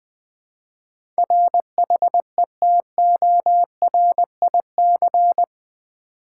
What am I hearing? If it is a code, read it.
Morse code, heard as a keyed tone, about 20 wpm, 700 Hz. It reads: RHETORIC
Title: Build A Word - 4 Letters At A Time - 8-Letter Words - 20wpm